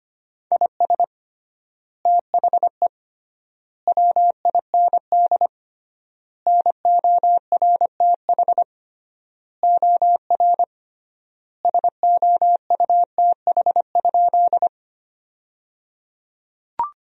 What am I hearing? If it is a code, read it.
Morse code, heard as a keyed tone, about 25 wpm, 700 Hz. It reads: IS THE WIND NORTH OR SOUTH?
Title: 3x Word-Spacing - Sentences from Top 400 Words - 25wpm